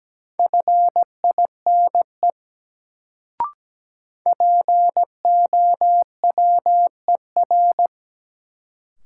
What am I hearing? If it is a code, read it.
Morse code, heard as a keyed tone, about 17 wpm, 700 Hz. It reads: FINE POWER
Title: Sentences from Top 500 Words - New Word Drill - 17wpm